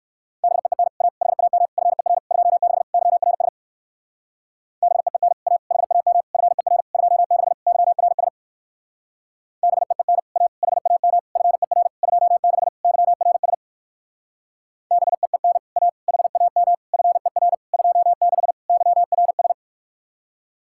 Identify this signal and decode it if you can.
Morse code, heard as a keyed tone, about 57 wpm, 700 Hz. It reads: BEEN A HAM FER 26 YRS BEEN A HAM FER 26 YRS BEEN A HAM FER 26 YRS BEEN A HAM FER 26 YRS